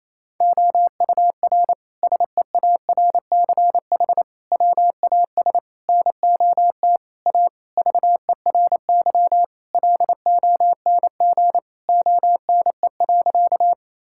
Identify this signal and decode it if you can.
Morse code, heard as a keyed tone, about 28 wpm, 700 Hz. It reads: OUR SEARCH WAS NOT A VERY LONG ONE.
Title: Sentences from Top 1000 Words - Review Part 5 - 28wpm